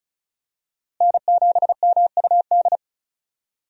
Morse code, heard as a keyed tone, about 36 wpm, 700 Hz. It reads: N7MUD
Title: USA Call Signs - 5 Characters - Repeated 2x - 35wpm